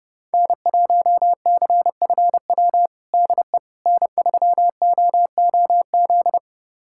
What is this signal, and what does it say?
Morse code, heard as a keyed tone, about 30 wpm, 700 Hz. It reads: N1CFW DE N3OOZ
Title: USA Callsign DE Callsign 30wpm